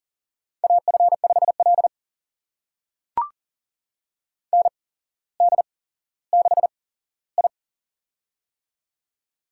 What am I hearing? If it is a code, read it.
Morse code, heard as a keyed tone, about 39 wpm, spaced as for 17 wpm, 700 Hz. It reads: AF5L ND6I
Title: USA Call Signs - 8x Character Spacing - 40wpm